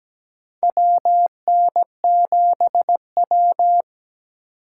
Morse code, heard as a keyed tone, about 17 wpm, 700 Hz. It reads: WN7W
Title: USA Call Signs - 4 Characters - 2x Character Spacing - 17wpm